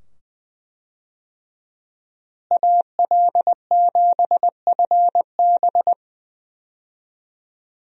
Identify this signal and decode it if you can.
Morse code, heard as a keyed tone, about 21 wpm, 700 Hz. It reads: AL7FB
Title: USA Call Signs - 5 Characters - Repeated 2x - 20wpm